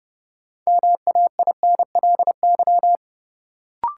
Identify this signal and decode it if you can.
Morse code, heard as a keyed tone, about 30 wpm, 700 Hz. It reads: MAINLY
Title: Sentences from Top 1400 Words - New Word Drill - 30wpm